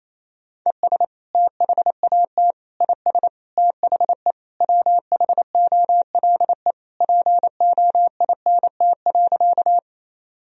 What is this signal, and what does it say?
Morse code, heard as a keyed tone, about 27 wpm, 700 Hz. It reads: ES THAT IS THE WHOLE POINT.
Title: Sentences from Top 400 Words and Abbreviations 28wpm